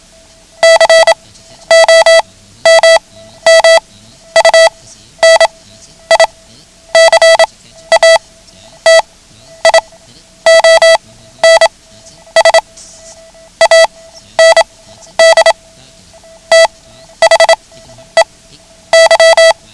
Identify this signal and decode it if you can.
Morse code, heard as a keyed tone, about 27 wpm, spaced as for 12 wpm, 660 Hz. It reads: COMMUNICATIONS AND THEY
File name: ham radio / CodeQuick / Code Quick 09